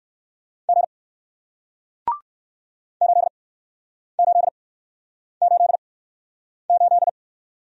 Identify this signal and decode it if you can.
Morse code, heard as a keyed tone, about 62 wpm, spaced as for 9 wpm, 700 Hz. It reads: G 8888